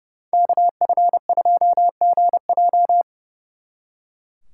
Morse code, heard as a keyed tone, about 34 wpm, 700 Hz. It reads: KF2GJ
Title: Speed Racing - USA Call Signs 30wpm